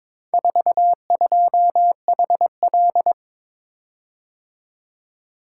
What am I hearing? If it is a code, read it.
Morse code, heard as a keyed tone, about 22 wpm, 700 Hz. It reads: 42HL
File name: practice / 22wpm / Four Letter-Number Combinations - 22wpm